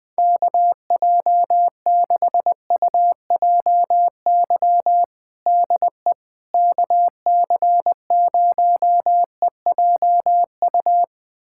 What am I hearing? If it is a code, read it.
Morse code, heard as a keyed tone, about 20 wpm, 700 Hz. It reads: KJ6UJY DE KC0EJU
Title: USA Callsign DE Callsign - 6 Characters - 20wpm